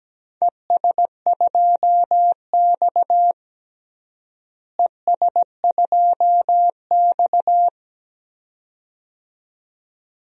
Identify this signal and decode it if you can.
Morse code, heard as a keyed tone, about 17 wpm, 700 Hz. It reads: ES2X ES2X
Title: DXCC-100 Call Signs - 4 Characters - Repeated 2x - 17wpm